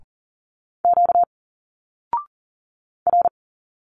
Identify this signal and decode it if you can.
Morse code, heard as a keyed tone, about 40 wpm, 700 Hz. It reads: Q R
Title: Single Letters 40wpm